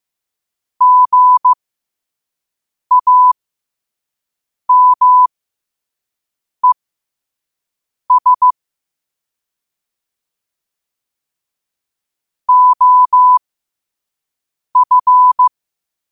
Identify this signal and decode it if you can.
Morse code, heard as a keyed tone, about 15 wpm, spaced as for 5 wpm, 1000 Hz. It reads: GAMES OF
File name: qotdfiles / 5wpm